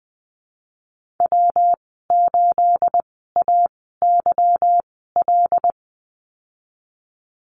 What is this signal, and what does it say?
Morse code, heard as a keyed tone, about 20 wpm, spaced as for 14 wpm, 700 Hz. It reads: W8AYL
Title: 245 Lesson 39g - Letter z - Call Signs 20wpm